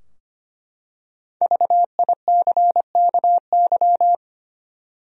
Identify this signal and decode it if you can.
Morse code, heard as a keyed tone, about 25 wpm, 700 Hz. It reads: VICKY
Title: USA Common Names - 25wpm